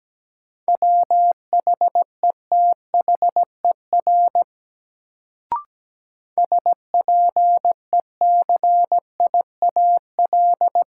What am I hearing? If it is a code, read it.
Morse code, heard as a keyed tone, about 17 wpm, 700 Hz. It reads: WHETHER SPECIAL
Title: Sentences from Top 1000 Words - Review Part 3 - New Word Drill - 17wpm